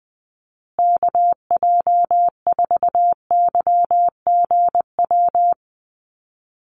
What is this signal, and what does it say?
Morse code, heard as a keyed tone, about 20 wpm, 700 Hz. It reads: KJ4YGW
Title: USA Call Signs Repeated 3x 20wpm